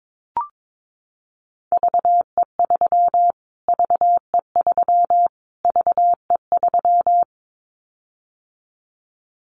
Morse code, heard as a keyed tone, about 22 wpm, 700 Hz. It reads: VE3 VE3 VE3
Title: Callsign Prefixes for Top 25 DXCC - Repeated 3x - 22wpm